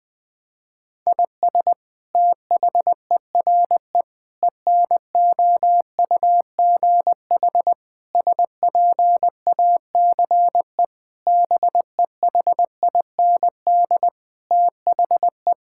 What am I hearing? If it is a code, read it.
Morse code, heard as a keyed tone, about 20 wpm, 700 Hz. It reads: IS THERE ENOUGH SPACE BEHIND THE
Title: Sentences from Top 1000 Words - Review Part 5 - 20wpm